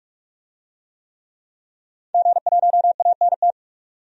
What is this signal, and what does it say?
Morse code, heard as a keyed tone, about 46 wpm, 700 Hz. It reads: G1ANT
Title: DXCC-100 Call Signs - 5 Characters - 45wpm